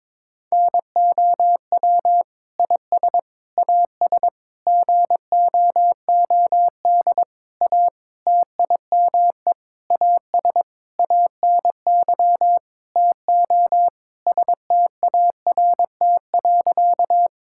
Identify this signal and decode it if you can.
Morse code, heard as a keyed tone, about 22 wpm, 700 Hz. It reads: NOW IS AS GOOD A TIME AS ANY TO START.
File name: Sentences with 9-Word Phrases - 22wpm